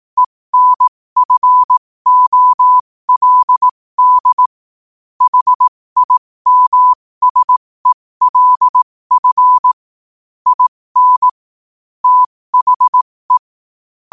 Morse code, heard as a keyed tone, about 18 wpm, spaced as for 15 wpm, 1000 Hz. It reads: ENFOLD HIMSELF IN THE
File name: qotdfiles / 15wpm